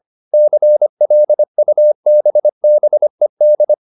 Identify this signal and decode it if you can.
Morse code, heard as a keyed tone, about 25 wpm, 600 Hz. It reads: CLUBBED
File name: the old man and the sea part 1 25wpm